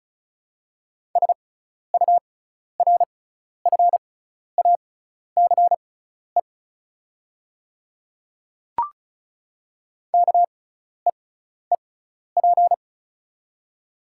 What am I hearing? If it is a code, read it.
Morse code, heard as a keyed tone, about 36 wpm, spaced as for 12 wpm, 690 Hz. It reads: SURFACE KEEP
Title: 6x Character Spacing - Top 500 Words - 35wpm